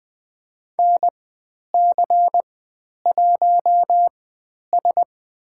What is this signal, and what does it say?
Morse code, heard as a keyed tone, about 20 wpm, spaced as for 10 wpm, 700 Hz. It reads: NC1S
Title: 095 Lesson 16g - Letter c - Call Signs 20wpm